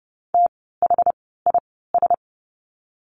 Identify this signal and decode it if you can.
Morse code, heard as a keyed tone, about 30 wpm, spaced as for 17 wpm, 700 Hz. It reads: THIS
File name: course / 30:10wpm / 108 Lesson 18f - Letter m - Words 30wpm